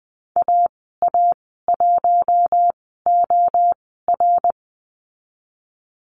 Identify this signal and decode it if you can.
Morse code, heard as a keyed tone, about 20 wpm, spaced as for 15 wpm, 700 Hz. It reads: AA1OR
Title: 082 Lesson 14g - Number 2 - Call Signs 20wpm